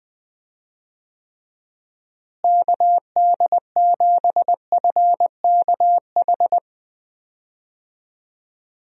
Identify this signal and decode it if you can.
Morse code, heard as a keyed tone, about 20 wpm, 700 Hz. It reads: KD7FKH
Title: USA Call Signs - 6 Characters - Repeated 2x - 20wpm